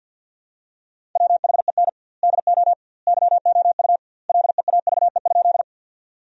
Morse code, heard as a keyed tone, about 50 wpm, 700 Hz. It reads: WHEN DO YOU LEAVE?
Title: 2x Word-Spacing - Sentences from Top 400 Words - 50wpm